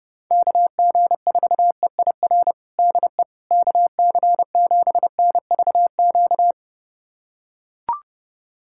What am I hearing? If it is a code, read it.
Morse code, heard as a keyed tone, about 32 wpm, 700 Hz. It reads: KG4EIR DE KC7NVQ
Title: USA Callsign DE Callsign 30wpm